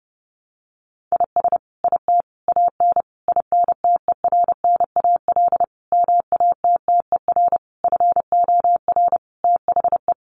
Morse code, heard as a keyed tone, about 30 wpm, 700 Hz. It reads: IS IT AN INTERNAL MATTER FOR THE
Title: Sentences from Top 1400 Words 30wpm